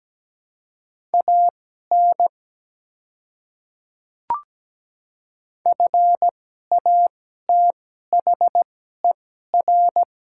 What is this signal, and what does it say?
Morse code, heard as a keyed tone, about 17 wpm, spaced as for 12 wpm, 700 Hz. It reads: AN FATHER
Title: 2x Character Spacing - Top 200 Words - 17wpm